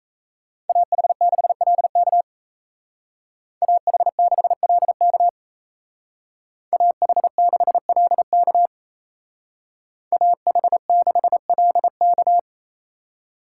Morse code, heard as a keyed tone, about 36 wpm, 700 Hz. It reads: AH6LK AH6LK AH6LK AH6LK